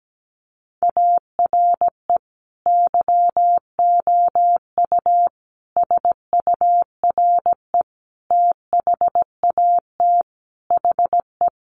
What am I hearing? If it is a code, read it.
Morse code, heard as a keyed tone, about 17 wpm, 700 Hz. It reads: ARE YOU SURE THAT HE
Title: Sentences from Top 1500 Words 17wpm